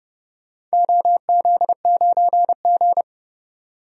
Morse code, heard as a keyed tone, about 30 wpm, 700 Hz. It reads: OZ9G
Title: DXCC-100 Call Signs - 4 Characters - 30wpm